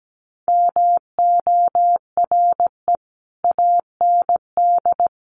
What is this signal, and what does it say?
Morse code, heard as a keyed tone, about 17 wpm, 700 Hz. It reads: MORE AND
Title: Sets of 3 Words 17wpm